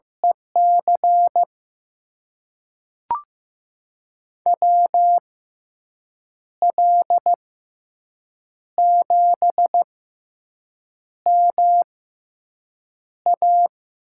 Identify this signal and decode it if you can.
Morse code, heard as a keyed tone, about 15 wpm, spaced as for 6 wpm, 700 Hz. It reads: EC WL7MA
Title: USA Call Signs - 5 Characters - 6x Character Spacing - 15wpm